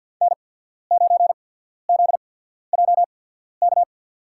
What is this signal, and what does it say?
Morse code, heard as a keyed tone, about 52 wpm, spaced as for 14 wpm, 700 Hz. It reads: N9ZJK
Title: USA Call Signs - 5 Characters - 8x Character Spacing - 50wpm